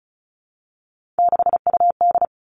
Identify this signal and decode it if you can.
Morse code, heard as a keyed tone, about 36 wpm, 700 Hz. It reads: 6UD
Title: Three Challenging Letter-Number Combinations 35wpm